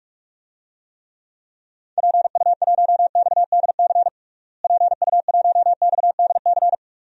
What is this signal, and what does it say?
Morse code, heard as a keyed tone, about 47 wpm, 700 Hz. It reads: PU1XDC PU1XDC
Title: DXCC-100 Call Signs - 6 Characters - Repeated 2x - 45wpm